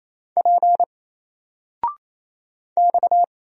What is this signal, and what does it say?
Morse code, heard as a keyed tone, about 29 wpm, 700 Hz. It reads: P X
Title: Single Letter-Number 28wpm